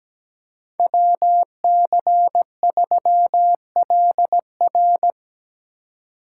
Morse code, heard as a keyed tone, about 20 wpm, 700 Hz. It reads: WC3LR
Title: Speed Racing - USA 2x2 Call Signs 17wpm